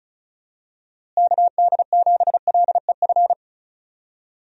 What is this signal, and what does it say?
Morse code, heard as a keyed tone, about 34 wpm, 700 Hz. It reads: KD7LEF
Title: USA Call Signs - 6 Characters - Repeated 2x - 35wpm